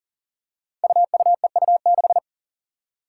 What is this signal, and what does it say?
Morse code, heard as a keyed tone, about 38 wpm, 700 Hz. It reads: UUEU6
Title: Five Challenging Letter-Number Combinations 40wpm